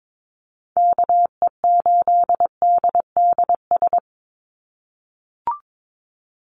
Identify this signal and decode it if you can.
Morse code, heard as a keyed tone, about 22 wpm, 700 Hz. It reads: KE8DDS
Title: USA Call Signs 22wpm